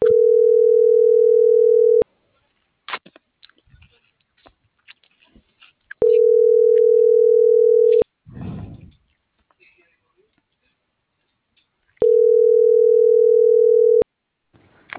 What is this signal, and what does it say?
Ringback tone (North America), 440 + 480 Hz.